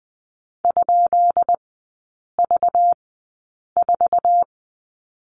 Morse code, heard as a keyed tone, about 20 wpm, spaced as for 8 wpm, 700 Hz. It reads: ?V4
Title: 191 Lesson 31e - Letter v - Three Characters 20wpm